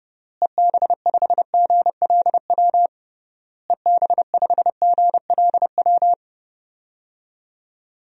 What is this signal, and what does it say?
Morse code, heard as a keyed tone, about 30 wpm, 700 Hz. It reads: EB5GLW EB5GLW
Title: DXCC-100 Call Signs - 6 Characters - Repeated 3x - 30wpm